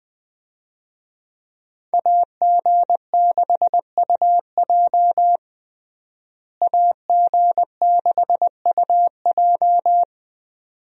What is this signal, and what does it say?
Morse code, heard as a keyed tone, about 20 wpm, 700 Hz. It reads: AG6UJ AG6UJ
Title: USA Call Signs - 5 Characters - Repeated 2x - 20wpm